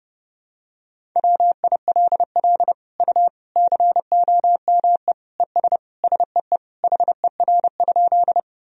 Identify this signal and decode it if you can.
Morse code, heard as a keyed tone, about 30 wpm, 700 Hz. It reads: WILL U COME ES SEE HER?
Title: Sentences from Top 100 Words and Abbreviations 30wpm